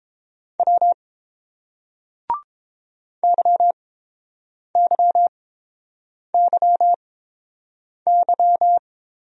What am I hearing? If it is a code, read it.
Morse code, heard as a keyed tone, about 27 wpm, spaced as for 7 wpm, 700 Hz. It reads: W YYYY
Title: Speed-Racing Single Letters 22wpm